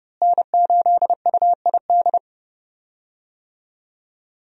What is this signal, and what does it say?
Morse code, heard as a keyed tone, about 32 wpm, 700 Hz. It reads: N8UID
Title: USA Call Signs - 5 Characters - Repeated 2x - 30wpm